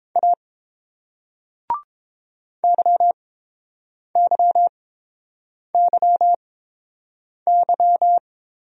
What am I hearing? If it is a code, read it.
Morse code, heard as a keyed tone, about 30 wpm, spaced as for 7 wpm, 700 Hz. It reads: A YYYY